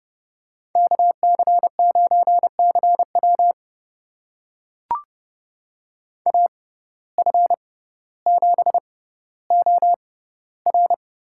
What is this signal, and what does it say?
Morse code, heard as a keyed tone, about 30 wpm, spaced as for 15 wpm, 690 Hz. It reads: KC9CW AF7OR
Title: USA Call Signs - 5 Characters - 6x Character Spacing - 30wpm